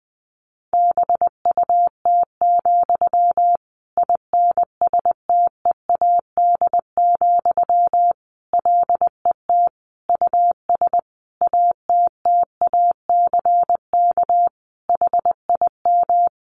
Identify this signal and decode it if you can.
Morse code, heard as a keyed tone, about 20 wpm, 700 Hz. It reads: BUT, INSTEAD, LET US ATTACK HIM
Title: Sentences from Top 1500 Words 20wpm